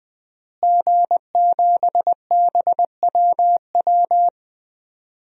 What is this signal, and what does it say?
Morse code, heard as a keyed tone, about 20 wpm, 700 Hz. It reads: G7BWW
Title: DXCC-100 Call Signs - 5 Characters - Repeated 2x - 20wpm